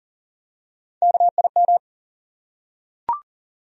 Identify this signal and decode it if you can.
Morse code, heard as a keyed tone, about 40 wpm, 700 Hz. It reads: KIM
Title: USA Common Names - Max 4 Letters - 40wpm